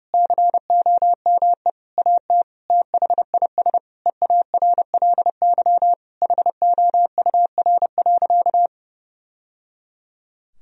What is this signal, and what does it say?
Morse code, heard as a keyed tone, about 32 wpm, 700 Hz. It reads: COME AT THIS EARLY HOUR.
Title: Sentences from Top 1000 Words - Review Part 2 - 30wpm